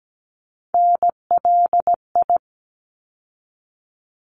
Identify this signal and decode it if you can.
Morse code, heard as a keyed tone, about 17 wpm, 700 Hz. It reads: NLI
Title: ARRL Sections 17wpm